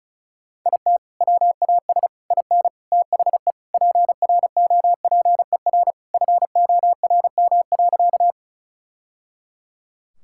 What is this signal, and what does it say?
Morse code, heard as a keyed tone, about 34 wpm, 700 Hz. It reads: IT WAS IN THE PROPER FORM.
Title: Sentences from Top 1000 Words - Review Part 5 - 35wpm